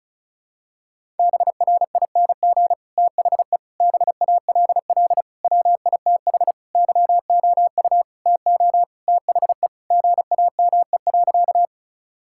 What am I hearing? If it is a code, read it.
Morse code, heard as a keyed tone, about 34 wpm, 700 Hz. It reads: BRING THE BALL WITH YOU TO THE GAME.
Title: Sentences from Top 1000 Words - Review Part 6 - 35wpm